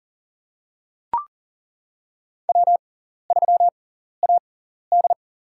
Morse code, heard as a keyed tone, about 37 wpm, spaced as for 14 wpm, 690 Hz. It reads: W3AD
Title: USA Call Signs - 6x Character Spacing - 40wpm